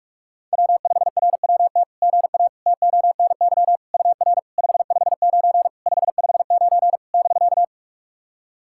Morse code, heard as a keyed tone, about 44 wpm, 690 Hz. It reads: W5RWT GA TONY UR 559 559 <BK>
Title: SOTA QSO Practice 45wpm